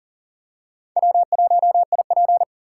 Morse code, heard as a keyed tone, about 40 wpm, 700 Hz. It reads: W1IP
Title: USA Call Signs - 4 Characters - 6x Character Spacing - 40wpm